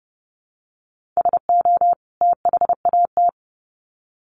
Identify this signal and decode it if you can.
Morse code, heard as a keyed tone, about 30 wpm, 700 Hz. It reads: SO THAT
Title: Sets of 2 Words 30wpm